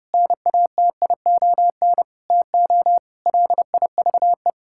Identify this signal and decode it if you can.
Morse code, heard as a keyed tone, about 30 wpm, 700 Hz. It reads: NATION TO LIVE
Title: Sentences from Top 1000 Words - Review Part 6 - 30wpm